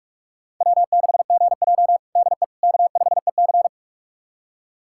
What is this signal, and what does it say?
Morse code, heard as a keyed tone, about 46 wpm, 700 Hz. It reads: W6GJ DE K5EC
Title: USA Callsign DE Callsign - 4 Characters - 45wpm